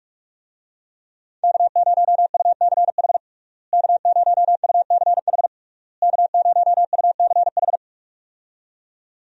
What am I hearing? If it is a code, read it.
Morse code, heard as a keyed tone, about 44 wpm, 700 Hz. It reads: K0UCH K0UCH K0UCH
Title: USA Call Signs - 5 Characters - Repeated 3x - 45wpm